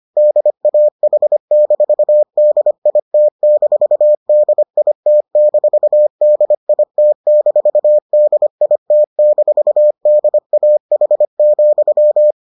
Morse code, heard as a keyed tone, about 25 wpm, 600 Hz. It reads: DAH-DIT-DIT-DIT-DIT-DAH,